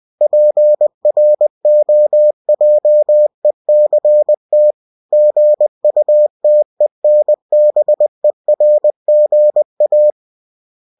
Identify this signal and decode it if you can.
Morse code, heard as a keyed tone, about 20 wpm, 600 Hz. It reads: PROJECT GUTENBERGA